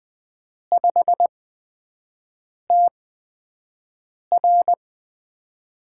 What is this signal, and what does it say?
Morse code, heard as a keyed tone, about 20 wpm, spaced as for 5 wpm, 700 Hz. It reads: 5TR
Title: USA Call Signs - 5 Characters - 8x Character Spacing - 20wpm